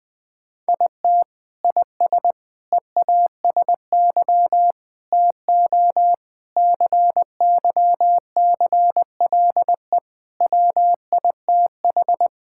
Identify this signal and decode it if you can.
Morse code, heard as a keyed tone, about 21 wpm, 700 Hz. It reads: IT IS EASY TO CYCLE WITH
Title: Sentences from Top 1000 Words - Review Part 6 - 20wpm